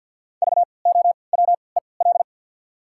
Morse code, heard as a keyed tone, about 52 wpm, spaced as for 29 wpm, 700 Hz. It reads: VOWEL
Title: 3x Character Spacing - Top 500 Words - 50wpm